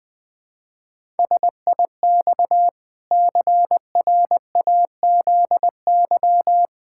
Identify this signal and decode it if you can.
Morse code, heard as a keyed tone, about 21 wpm, 700 Hz. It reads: SIX CRAZY
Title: Pangram Challenge v2 - 20wpm